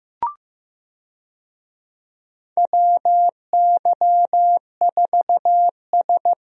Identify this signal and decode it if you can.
Morse code, heard as a keyed tone, about 15 wpm, 700 Hz. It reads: WY4S
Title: USA Call Signs - 4 Characters - Repeated 2x - 15wpm